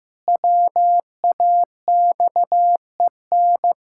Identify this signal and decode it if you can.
Morse code, heard as a keyed tone, about 15 wpm, 700 Hz. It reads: WAXEN
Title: Pangram Challenge v2 - 15wpm